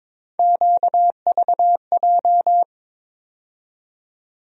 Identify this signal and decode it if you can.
Morse code, heard as a keyed tone, about 22 wpm, 700 Hz. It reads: QVJ
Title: USA Call Sign Postfixes - 3 Letters 22wpm